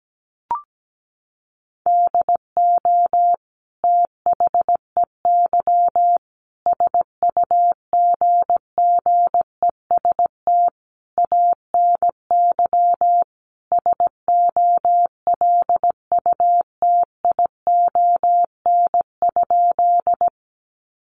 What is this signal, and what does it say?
Morse code, heard as a keyed tone, about 17 wpm, 700 Hz. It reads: DO THEY SUGGEST ANY SOLUTION?
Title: Sentences from Top 1300 Words 17wpm